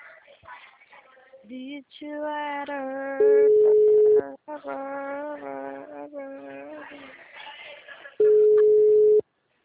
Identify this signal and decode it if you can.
Ringback tone (Europe, 425 Hz), with speech.
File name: out- -2012- - - .95762